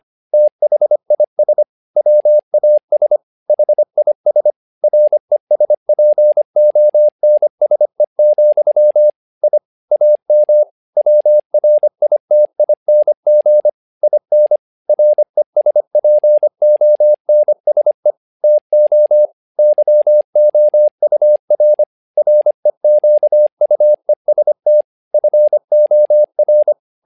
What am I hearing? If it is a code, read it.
Morse code, heard as a keyed tone, about 25 wpm, 600 Hz. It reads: THIS WAS HIS RESPONSE, I AM WRITING IN RESPONSE TO YOUR REQUEST FOR
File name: learning center / tower trouble / tower trouble 25wpm